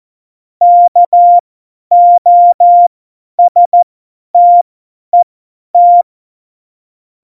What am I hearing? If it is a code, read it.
Morse code, heard as a keyed tone, about 14 wpm, spaced as for 10 wpm, 700 Hz. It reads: KOSTET
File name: Morsekurs / Witze / Witz 16. Oktober 2018, 50 F70